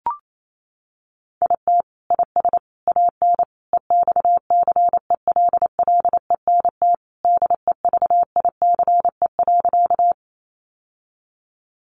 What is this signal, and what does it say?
Morse code, heard as a keyed tone, about 28 wpm, 700 Hz. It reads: IT IS AN EXCELLENT DEVICE.
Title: Sentences from Top 1500 Words 28wpm